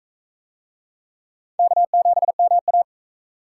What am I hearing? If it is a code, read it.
Morse code, heard as a keyed tone, about 39 wpm, 700 Hz. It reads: K7MA